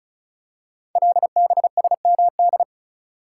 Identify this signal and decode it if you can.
Morse code, heard as a keyed tone, about 36 wpm, 700 Hz. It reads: LBSMD
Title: Code Groups - 5 Letters - 35wpm